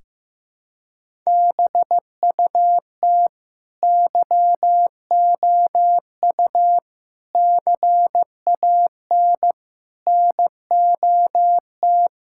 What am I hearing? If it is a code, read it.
Morse code, heard as a keyed tone, about 15 wpm, 700 Hz. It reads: BUT YOU CAN NOT
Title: Sentences from Top 1000 Words - Review Part 6 - 15wpm